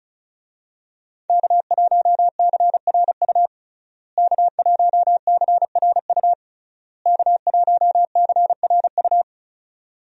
Morse code, heard as a keyed tone, about 36 wpm, 700 Hz. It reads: K1CRU K1CRU K1CRU
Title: USA Call Signs - 5 Characters - Repeated 3x - 35wpm